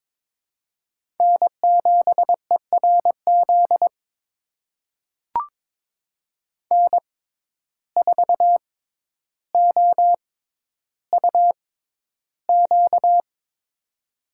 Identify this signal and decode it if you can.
Morse code, heard as a keyed tone, about 22 wpm, spaced as for 11 wpm, 700 Hz. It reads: N7ERZ N4OUQ
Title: USA Call Signs - 5 Characters - 6x Character Spacing - 22wpm